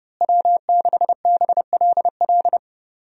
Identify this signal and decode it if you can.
Morse code, heard as a keyed tone, about 32 wpm, 700 Hz. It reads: W6BLL
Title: USA Call Signs - 5 Characters - Repeated 2x - 30wpm